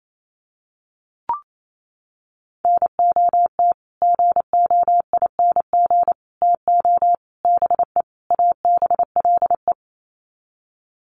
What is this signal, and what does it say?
Morse code, heard as a keyed tone, about 28 wpm, 700 Hz. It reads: NOT GOING TO BE ABLE
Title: Sets of 5 Words 28wpm